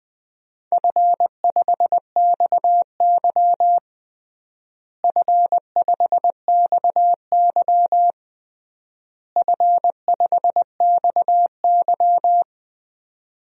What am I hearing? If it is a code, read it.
Morse code, heard as a keyed tone, about 21 wpm, 700 Hz. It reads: F5XY F5XY F5XY
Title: DXCC-100 Call Signs - 4 Characters - Repeated 3x - 20wpm